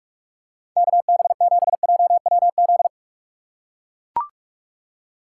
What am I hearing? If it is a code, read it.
Morse code, heard as a keyed tone, about 42 wpm, 700 Hz. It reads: KB7JWZ
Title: USA Call Signs - 6 Characters - 3x Character Spacing - 45wpm